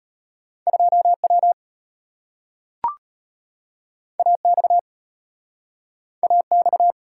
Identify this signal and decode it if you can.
Morse code, heard as a keyed tone, about 34 wpm, 700 Hz. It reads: 2W AX AX